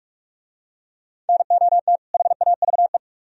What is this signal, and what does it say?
Morse code, heard as a keyed tone, about 43 wpm, 700 Hz. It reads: NOT HAVE